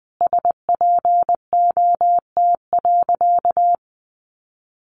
Morse code, heard as a keyed tone, about 20 wpm, 700 Hz. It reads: SPOT.
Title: Sentences from Top 1100 Words 20wpm